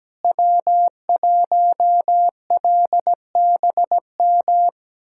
Morse code, heard as a keyed tone, about 17 wpm, 700 Hz. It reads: W1LBM
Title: USA Callsign DE Callsign - 5 Characters - 17wpm